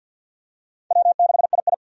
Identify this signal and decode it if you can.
Morse code, heard as a keyed tone, about 47 wpm, 700 Hz. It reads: W6II
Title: USA Call Signs - 4 Characters - 50wpm